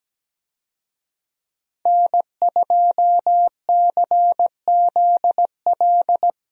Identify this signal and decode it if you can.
Morse code, heard as a keyed tone, about 17 wpm, 700 Hz. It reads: N2CZL
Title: USA Call Signs - 5 Characters - Repeated 2x - 17wpm